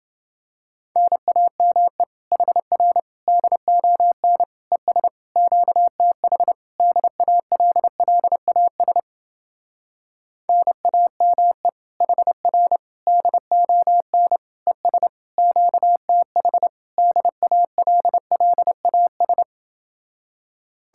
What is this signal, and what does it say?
Morse code, heard as a keyed tone, about 29 wpm, 700 Hz. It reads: NAME HR DON ES QTH DALLAS NAME HR DON ES QTH DALLAS